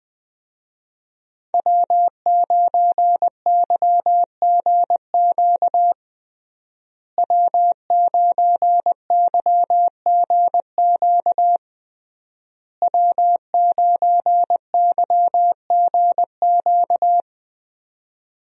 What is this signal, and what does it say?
Morse code, heard as a keyed tone, about 20 wpm, 700 Hz. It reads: W9YGQ W9YGQ W9YGQ
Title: USA Call Signs - 5 Characters - Repeated 3x - 20wpm